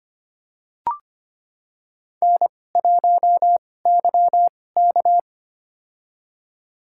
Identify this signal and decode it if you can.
Morse code, heard as a keyed tone, about 25 wpm, spaced as for 18 wpm, 700 Hz. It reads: N1YK
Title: USA Call Signs - 2x Character Spacing - 25wpm